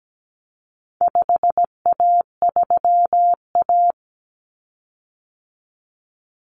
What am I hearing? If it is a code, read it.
Morse code, heard as a keyed tone, about 17 wpm, 700 Hz. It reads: 5A3A
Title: Common Cut Numbers - 4-Digit 17wpm